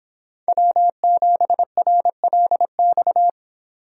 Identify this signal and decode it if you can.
Morse code, heard as a keyed tone, about 24 wpm, 700 Hz. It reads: W7RLX